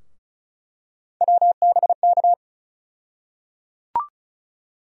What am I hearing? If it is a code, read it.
Morse code, heard as a keyed tone, about 36 wpm, 700 Hz. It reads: WBK
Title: USA Call Sign Postfixes - 3 Letters 35wpm